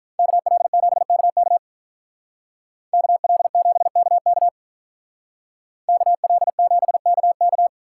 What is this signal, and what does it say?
Morse code, heard as a keyed tone, about 43 wpm, 700 Hz. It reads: KL7KK KL7KK KL7KK
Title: Speed Racing - USA Call Signs 35wpm